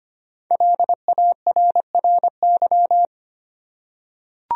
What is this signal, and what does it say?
Morse code, heard as a keyed tone, about 25 wpm, 700 Hz. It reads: LARRY
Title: USA Common Names - Max 5 Letters - 25wpm